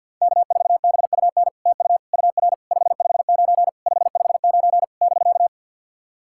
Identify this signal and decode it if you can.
Morse code, heard as a keyed tone, about 48 wpm, 700 Hz. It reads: K4BUN TU UR 559 559 <BK>
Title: SOTA QSO Practice 50wpm